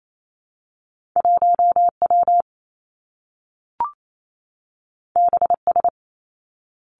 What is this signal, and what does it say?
Morse code, heard as a keyed tone, about 28 wpm, 700 Hz. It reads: 1 W B S
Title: Two Challenging Letter-Number Combinations 28wpm